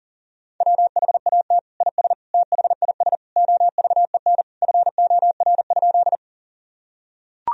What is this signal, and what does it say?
Morse code, heard as a keyed tone, about 40 wpm, 700 Hz. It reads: WHAT IS THIS OVEN FOR?
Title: Sentences from Top 1000 Words - Review Part 4 - 40wpm